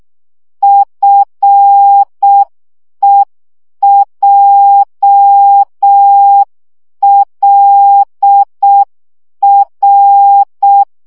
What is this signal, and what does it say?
Morse code, heard as a keyed tone, about 6 wpm, 800 Hz. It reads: FEJLR
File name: morze vizsga 2011-3 5